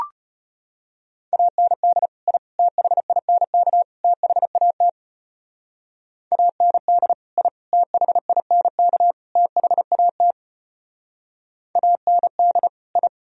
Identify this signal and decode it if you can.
Morse code, heard as a keyed tone, about 30 wpm, 700 Hz. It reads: AND I THINK THAT AND I THINK THAT AND I
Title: Speed-Racing - Sets of 4 Words 25wpm